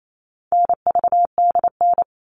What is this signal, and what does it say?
Morse code, heard as a keyed tone, about 28 wpm, 700 Hz. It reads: NVDN
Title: All Cut Numbers - 4-Digit 28wpm